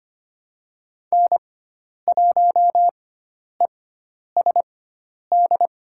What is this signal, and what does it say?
Morse code, heard as a keyed tone, about 25 wpm, spaced as for 10 wpm, 700 Hz. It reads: N1ESD
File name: Lesson 16g - Letter c - Call Signs 25wpm